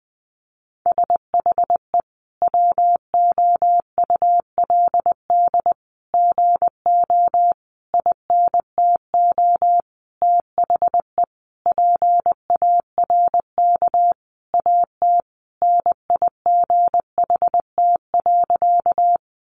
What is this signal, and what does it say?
Morse code, heard as a keyed tone, about 19 wpm, 700 Hz. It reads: SHE WOULD GO INTO THE PARK AT NIGHT.
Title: Sentences from Top 1300 Words 20wpm